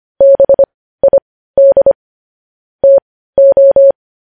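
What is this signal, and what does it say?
Morse code, heard as a keyed tone, about 25 wpm, spaced as for 15 wpm, 550 Hz. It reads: BID TO
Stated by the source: Morse code news headlines practice at 15, preceded by vvv.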